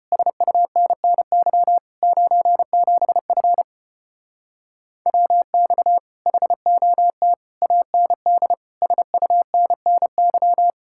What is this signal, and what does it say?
Morse code, heard as a keyed tone, about 33 wpm, 700 Hz. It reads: SUNNY 97F WX HOT AND SUNNY